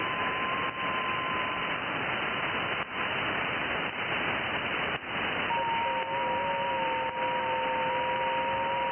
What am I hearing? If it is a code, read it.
FT8.
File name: IC-7100